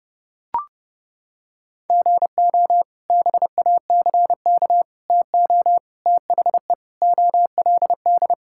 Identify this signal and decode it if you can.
Morse code, heard as a keyed tone, about 32 wpm, 700 Hz. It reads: GO BACK TO THE OLD
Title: Sentences from Top 1000 Words - Review Part 5 - 30wpm